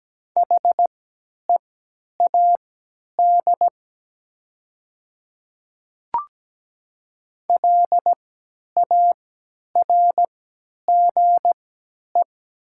Morse code, heard as a keyed tone, about 17 wpm, spaced as for 10 wpm, 700 Hz. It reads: HEAD LARGE
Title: 3x Character Spacing - Top 300 Words - 17wpm